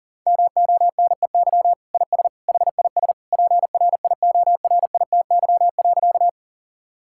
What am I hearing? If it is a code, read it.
Morse code, heard as a keyed tone, about 40 wpm, 700 Hz. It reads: MONEY IS HIS PRIORITY.
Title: Sentences from Top 1600 Words 40wpm